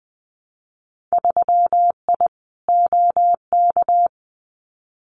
Morse code, heard as a keyed tone, about 20 wpm, 700 Hz. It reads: 3I OK
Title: Winter Field Day Exchanges Encore 20wpm